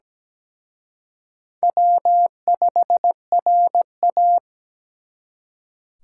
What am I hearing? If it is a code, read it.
Morse code, heard as a keyed tone, about 20 wpm, 700 Hz. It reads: W5RA
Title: Speed Racing - USA 1x2 Call Signs 17wpm